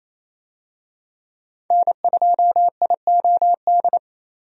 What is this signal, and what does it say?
Morse code, heard as a keyed tone, about 28 wpm, 700 Hz. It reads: N2IOD
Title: USA Call Signs - 5 Characters - 28wpm